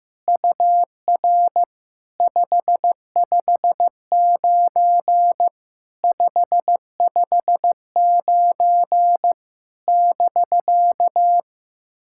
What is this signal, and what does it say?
Morse code, heard as a keyed tone, about 15 wpm, 700 Hz. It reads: UR 559 559 <BK>
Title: SOTA QSO Practice 15wpm